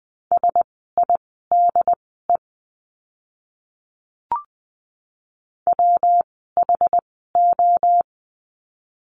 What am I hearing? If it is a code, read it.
Morse code, heard as a keyed tone, about 20 wpm, spaced as for 14 wpm, 700 Hz. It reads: SIDE WHO
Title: 115 Lesson 19f - Letter w - Words 20wpm